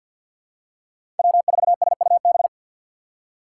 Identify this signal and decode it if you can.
Morse code, heard as a keyed tone, about 49 wpm, 700 Hz. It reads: W4SUB
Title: USA Call Signs - 4x Character Spacing - 50wpm